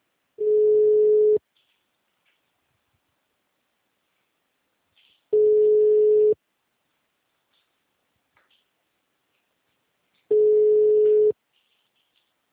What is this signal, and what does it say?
Ringback tone (Europe, 425 Hz).
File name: out- -2017- - - .58541